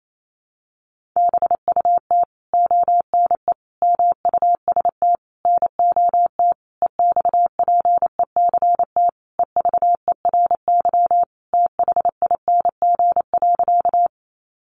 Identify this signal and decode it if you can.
Morse code, heard as a keyed tone, about 28 wpm, 700 Hz. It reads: BUT ONE MUST NOT EXPECT EVERYTHING.
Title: Sentences from Top 1100 Words 28wpm